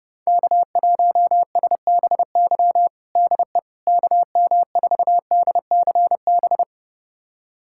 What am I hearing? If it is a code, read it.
Morse code, heard as a keyed tone, about 30 wpm, 700 Hz. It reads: K1SBY DE KM4DCB
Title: USA Callsign DE Callsign 30wpm